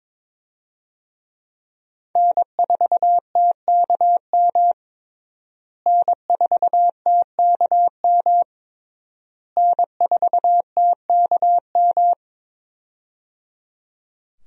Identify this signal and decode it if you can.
Morse code, heard as a keyed tone, about 22 wpm, 700 Hz. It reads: N4TKM N4TKM N4TKM
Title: USA Call Signs - 5 Characters - Repeated 3x - 22wpm